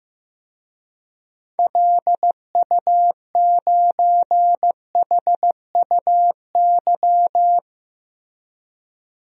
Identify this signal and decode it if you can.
Morse code, heard as a keyed tone, about 15 wpm, 690 Hz. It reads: LU9HUY
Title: DXCC-100 Call Signs - 6 Characters - 15wpm